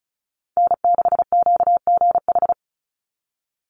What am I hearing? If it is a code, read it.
Morse code, heard as a keyed tone, about 34 wpm, 700 Hz. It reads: N6QGH
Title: USA Call Signs 35wpm